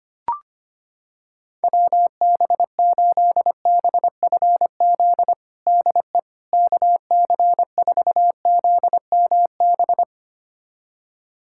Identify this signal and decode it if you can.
Morse code, heard as a keyed tone, about 25 wpm, 700 Hz. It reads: WB8BFZ DE KC4ZMB
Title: USA Callsign DE Callsign - 6 Characters - 25wpm